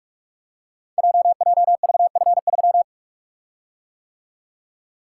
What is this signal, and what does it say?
Morse code, heard as a keyed tone, about 44 wpm, 690 Hz. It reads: JJVF3
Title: Code Groups - 5 Letter:Numbers - 45wpm